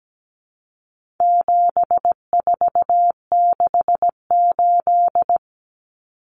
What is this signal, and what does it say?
Morse code, heard as a keyed tone, about 17 wpm, 700 Hz. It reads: 7468
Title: Numbers - 4-Digit 17wpm